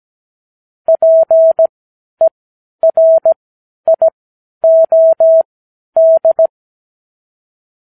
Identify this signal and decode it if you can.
Morse code, heard as a keyed tone, about 17 wpm, spaced as for 10 wpm, 650 Hz. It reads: PERIOD
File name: k7qo / Lesson 59 Period